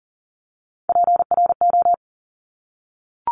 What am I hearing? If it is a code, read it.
Morse code, heard as a keyed tone, about 40 wpm, 690 Hz. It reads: PRO
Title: Top Three Letter Combinations in Words 40wpm